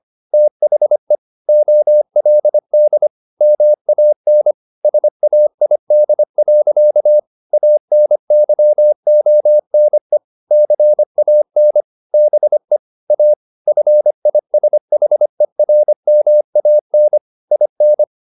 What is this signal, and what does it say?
Morse code, heard as a keyed tone, about 25 wpm, 600 Hz. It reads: THE OLD MAN SAID. ANYONE CAN BE A FISHERMAN IN
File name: the old man and the sea part 1 25wpm